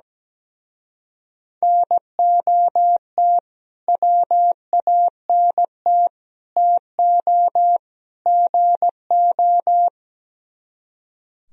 Morse code, heard as a keyed tone, about 17 wpm, 700 Hz. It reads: NOT WANT TO GO
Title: Speed-Racing - Sets of 4 Words 17wpm